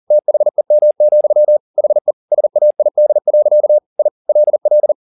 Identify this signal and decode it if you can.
Morse code, heard as a keyed tone, about 40 wpm, 600 Hz. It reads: THEM, HE SAID. I LL
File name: the old man and the sea 2 40wpm